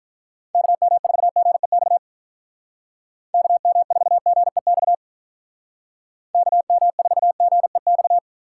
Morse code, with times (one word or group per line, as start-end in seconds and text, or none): 0.6-2.0 KM4GEX
3.3-4.9 KM4GEX
6.3-8.2 KM4GEX